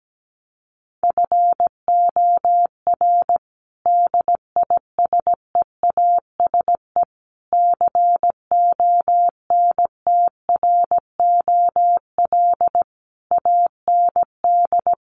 Morse code, heard as a keyed tone, about 17 wpm, 700 Hz. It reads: FOR DISEASE CONTROL AND
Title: Sets of 5 Words Encore2 17wpm